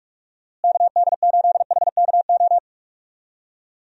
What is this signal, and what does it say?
Morse code, heard as a keyed tone, about 42 wpm, 700 Hz. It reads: KD8HKO
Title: USA Call Signs - 6 Characters - Repeated 3x - 45wpm